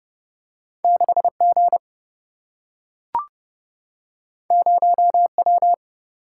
Morse code, heard as a keyed tone, about 30 wpm, 700 Hz. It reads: 6 G 0 W
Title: Single-Letter Single-Number 30wpm